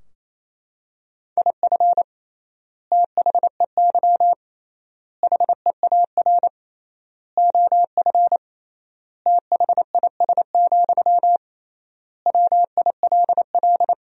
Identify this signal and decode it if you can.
Morse code, heard as a keyed tone, about 28 wpm, 700 Hz. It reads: IF THEY HEAR OF THIS, WILL
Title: 3x Word-Spacing - Sentences from Top 400 Words - 28wpm